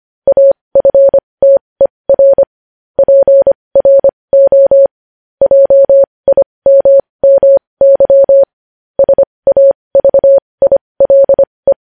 Morse code, heard as a keyed tone, about 25 wpm, spaced as for 20 wpm, 550 Hz. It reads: AFTER PRO JIMMY SAVILE